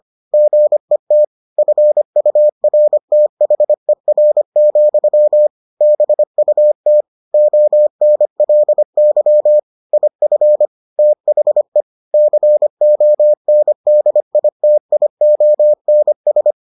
Morse code, heard as a keyed tone, about 25 wpm, 600 Hz. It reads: GET FURTHER, BUT ONLY IF THE CONDITIONS